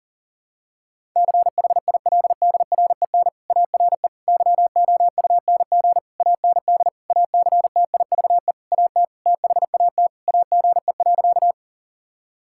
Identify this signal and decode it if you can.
Morse code, heard as a keyed tone, about 40 wpm, 700 Hz. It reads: CHILDREN ARE YOUNG AND ACTIVE AT THAT AGE.
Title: Sentences from Top 1000 Words - Review Part 6 - 40wpm